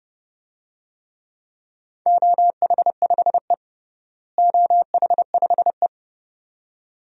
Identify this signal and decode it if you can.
Morse code, heard as a keyed tone, about 30 wpm, 700 Hz. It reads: OH5E OH5E
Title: DXCC-100 Call Signs - 4 Characters - Repeated 2x - 30wpm